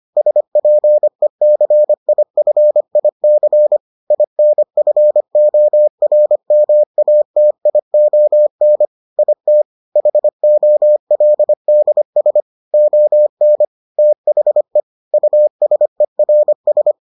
Morse code, heard as a keyed tone, about 25 wpm, 600 Hz. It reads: SPECIFIC INFORMATION IT HOLDS ON THE USERS